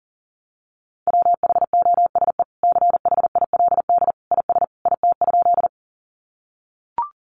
Morse code, heard as a keyed tone, about 40 wpm, 700 Hz. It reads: WHOSE CHILD IS IT?
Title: Sentences from Top 1200 Words 40wpm